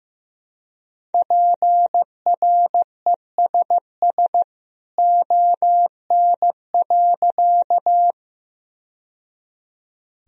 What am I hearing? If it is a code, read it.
Morse code, heard as a keyed tone, about 15 wpm, 700 Hz. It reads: PRESS ON.
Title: Sentences from Top 1000 Words - Review Part 1 - 15wpm